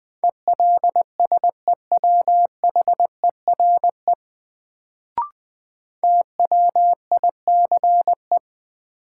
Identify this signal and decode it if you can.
Morse code, heard as a keyed tone, about 20 wpm, 700 Hz. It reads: ELSEWHERE TWICE
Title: Sentences from Top 1600 Words - New Word Drill - 20wpm